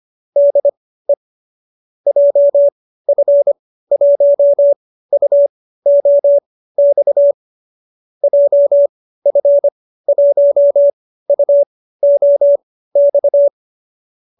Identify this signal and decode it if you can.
Morse code, heard as a keyed tone, about 25 wpm, spaced as for 15 wpm, 590 Hz. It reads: DE JF1UOX JF1UOX